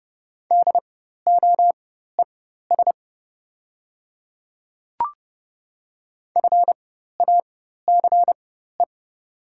Morse code, heard as a keyed tone, about 30 wpm, spaced as for 14 wpm, 700 Hz. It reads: DOES FACE
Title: 4x Character Spacing - Top 300 Words - 30wpm